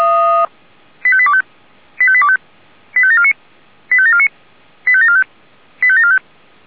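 Five-tone selective call (ZVEI), heard as quick runs of five tones, tones of about 70 ms. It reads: 85825 85825 85749 85749 85745 85745
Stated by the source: ZVEI five-tone selective call received by SDR.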